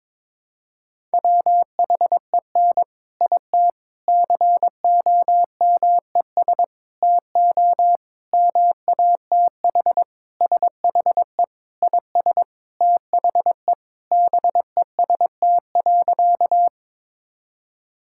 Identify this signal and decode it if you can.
Morse code, heard as a keyed tone, about 22 wpm, 700 Hz. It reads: WHEN IT COMES TO MATH SHE IS THE BEST.
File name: practice / 22wpm / Sentences with 4-Word Phrases - 22wpm